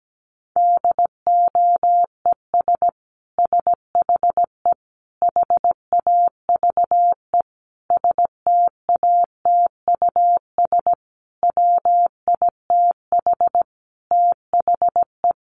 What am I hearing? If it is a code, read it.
Morse code, heard as a keyed tone, about 17 wpm, 700 Hz. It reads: DOES SHE HAVE STATUS WITH THE
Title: Sentences from Top 1400 Words 17wpm